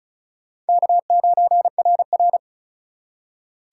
Morse code, heard as a keyed tone, about 48 wpm, spaced as for 42 wpm, 700 Hz. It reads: K9RR
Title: Speed Racing - USA Call Signs 35wpm